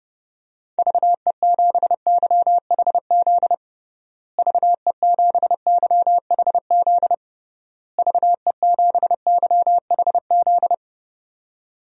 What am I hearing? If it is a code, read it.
Morse code, heard as a keyed tone, about 32 wpm, 700 Hz. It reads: VE7YHZ VE7YHZ VE7YHZ
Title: DXCC-100 Call Signs - 6 Characters - Repeated 3x - 30wpm